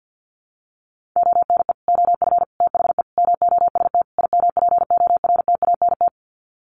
Morse code, heard as a keyed tone, about 50 wpm, 700 Hz. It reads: ONE OF THE MOST IMPORTANT
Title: Sets of 5 Words 50wpm